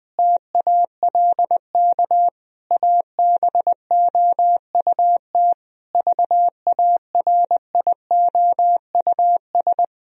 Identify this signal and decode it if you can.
Morse code, heard as a keyed tone, about 20 wpm, 700 Hz. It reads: TALK ABOUT VARIOUS
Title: Sentences from Top 1000 Words - Review Part 4 - 20wpm